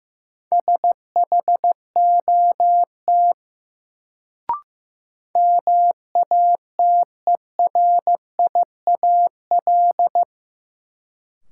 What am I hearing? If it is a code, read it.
Morse code, heard as a keyed tone, about 15 wpm, 700 Hz. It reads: SHOT MATERIAL
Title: Sentences from Top 1100 Words - New Word Drill Alt - 15wpm